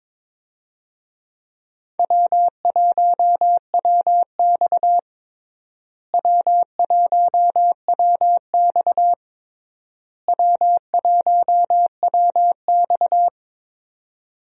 Morse code, heard as a keyed tone, about 23 wpm, 700 Hz. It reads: W1WX W1WX W1WX
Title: USA Call Signs - 4 Characters - Repeated 3x - 22wpm